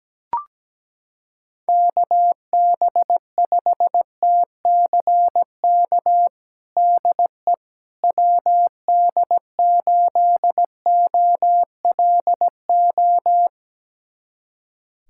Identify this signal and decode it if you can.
Morse code, heard as a keyed tone, about 17 wpm, 700 Hz. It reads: KB5TCK DE WD8OLO
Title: USA Callsign DE Callsign - 6 Characters - 17wpm